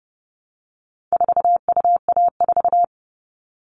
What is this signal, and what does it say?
Morse code, heard as a keyed tone, about 30 wpm, 700 Hz. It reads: 4UA4
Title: All Cut Numbers - 4-Digit 30wpm